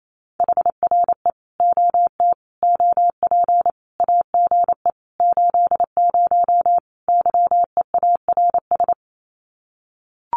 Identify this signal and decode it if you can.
Morse code, heard as a keyed tone, about 28 wpm, 700 Hz. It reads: HRE OT OP AGE 80 YEARS
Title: Common QSO Phrases 28wpm